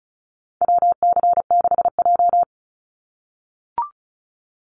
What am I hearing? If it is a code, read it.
Morse code, heard as a keyed tone, about 34 wpm, 700 Hz. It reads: WC6J
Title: NA SOTA Activator Callsigns 35wpm